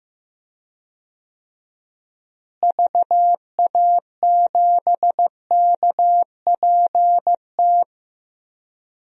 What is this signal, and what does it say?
Morse code, heard as a keyed tone, about 15 wpm, 700 Hz. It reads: VA7KPT
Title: DXCC-100 Call Signs - 6 Characters - Repeated 2x - 15wpm